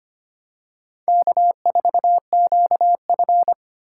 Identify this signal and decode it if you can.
Morse code, heard as a keyed tone, about 25 wpm, 700 Hz. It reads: K4QF
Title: USA Call Signs - 8x Character Spacing - 25wpm